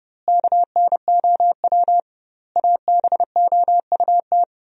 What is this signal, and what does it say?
Morse code, heard as a keyed tone, about 30 wpm, 700 Hz. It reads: KNOW ABOUT
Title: 2x Word-Spacing - Sentences from Top 100 Words - 30wpm